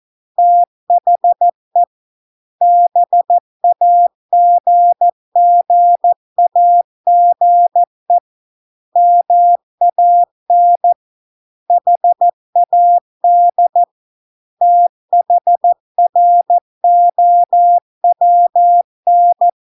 Morse code, heard as a keyed tone, about 14 wpm, 700 Hz. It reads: THE BAGGAGE MAN HAD THROWN